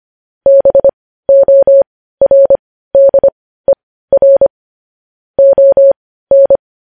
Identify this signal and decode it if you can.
Morse code, heard as a keyed tone, about 25 wpm, spaced as for 15 wpm, 550 Hz. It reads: BORDER ON
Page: news-headlines-in-morse-code-at-15-wpm